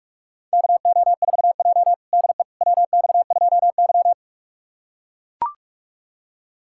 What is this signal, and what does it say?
Morse code, heard as a keyed tone, about 44 wpm, 700 Hz. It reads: KO4J DE WX2Y
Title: USA Callsign DE Callsign - 4 Characters - 45wpm